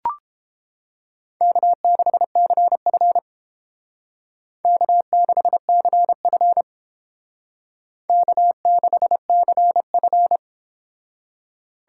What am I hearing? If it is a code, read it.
Morse code, heard as a keyed tone, about 31 wpm, 700 Hz. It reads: K6CF K6CF K6CF